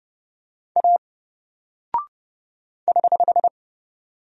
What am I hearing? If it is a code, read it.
Morse code, heard as a keyed tone, about 30 wpm, 700 Hz. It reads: A <HH>
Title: Single Character (Letter:Number:Punctuation:Prosign) 30wpm